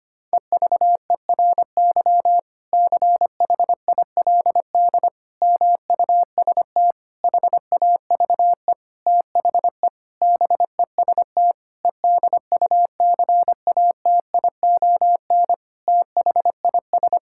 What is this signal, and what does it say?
Morse code, heard as a keyed tone, about 25 wpm, 700 Hz. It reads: EVERY CHILD MUST HAVE THE BEST EDUCATION THIS
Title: Sentences from Top 1000 Words - Review Part 6 - 25wpm